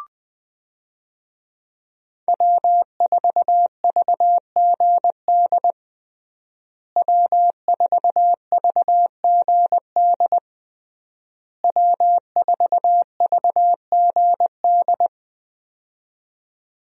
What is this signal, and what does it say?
Morse code, heard as a keyed tone, about 20 wpm, 700 Hz. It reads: W4VGD W4VGD W4VGD
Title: USA Call Signs - 5 Characters - Repeated 3x - 20wpm